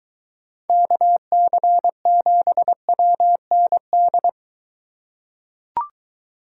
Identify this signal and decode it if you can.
Morse code, heard as a keyed tone, about 22 wpm, 700 Hz. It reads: KC7WND